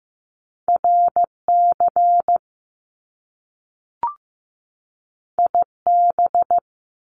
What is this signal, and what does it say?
Morse code, heard as a keyed tone, about 15 wpm, 700 Hz. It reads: RC IB
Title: Top Two Letter Combinations in Words 15wpm